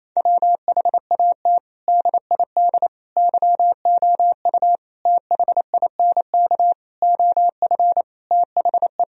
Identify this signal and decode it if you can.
Morse code, heard as a keyed tone, about 28 wpm, 700 Hz. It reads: WHAT DID YOU THINK OF THE
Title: Sentences from Top 1000 Words - Review Part 4 - 28wpm